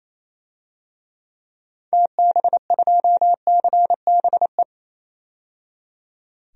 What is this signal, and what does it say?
Morse code, heard as a keyed tone, about 28 wpm, 700 Hz. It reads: TB2CBE
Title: DXCC-100 Call Signs - 6 Characters - 28wpm